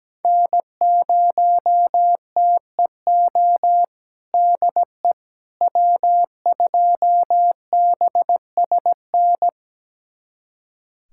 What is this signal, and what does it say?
Morse code, heard as a keyed tone, about 17 wpm, 700 Hz. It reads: N0TEO DE W2BSN
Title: USA Callsign DE Callsign - 5 Characters - 17wpm